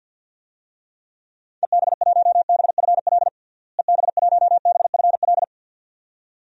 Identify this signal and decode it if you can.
Morse code, heard as a keyed tone, about 53 wpm, 700 Hz. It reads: EB1BFL EB1BFL
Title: DXCC-100 Call Signs - 6 Characters - Repeated 2x - 50wpm